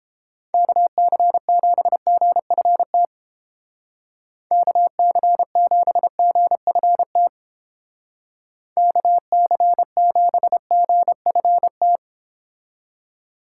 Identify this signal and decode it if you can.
Morse code, heard as a keyed tone, about 30 wpm, 700 Hz. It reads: KC7GFT KC7GFT KC7GFT